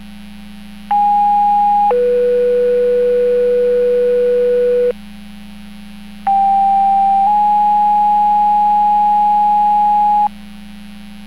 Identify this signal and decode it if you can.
Two-tone page, heard as a one-second tone, then a three-second tone, about 820 then 490 Hz, about 790 then 830 Hz. It is heard from 0.9 to 10.3 s.